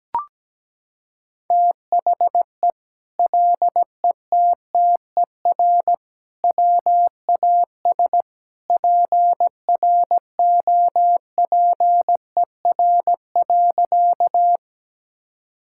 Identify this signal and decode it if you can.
Morse code, heard as a keyed tone, about 17 wpm, 700 Hz. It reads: THE LETTER WAS PROPER.
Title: Sentences from Top 1000 Words - Review Part 5 - 17wpm